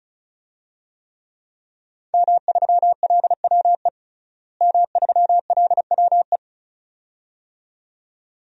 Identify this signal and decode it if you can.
Morse code, heard as a keyed tone, about 36 wpm, 700 Hz. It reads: M3LWE M3LWE
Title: DXCC-100 Call Signs - 5 Characters - Repeated 2x - 35wpm